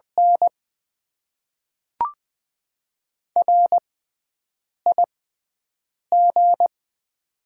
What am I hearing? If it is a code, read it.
Morse code, heard as a keyed tone, about 20 wpm, spaced as for 7 wpm, 690 Hz. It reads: N RIG
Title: Top 20 QSO Elements - 6x Character Spacing - 20wpm